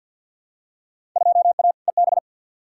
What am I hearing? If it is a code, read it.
Morse code, heard as a keyed tone, about 53 wpm, 700 Hz. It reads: 2A EB
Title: Field Day Exchanges - 50wpm